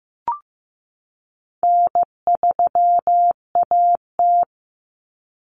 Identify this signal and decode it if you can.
Morse code, heard as a keyed tone, about 15 wpm, 700 Hz. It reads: N3AT
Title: Common Cut Numbers - 4-Digit 15wpm